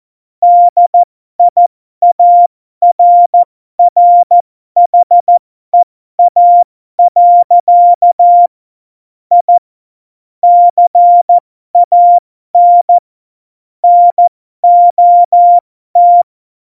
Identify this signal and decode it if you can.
Morse code, heard as a keyed tone, about 14 wpm, spaced as for 12 wpm, 700 Hz. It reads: DIARRHEA. I CAN NOT
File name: Morsekurs / Witze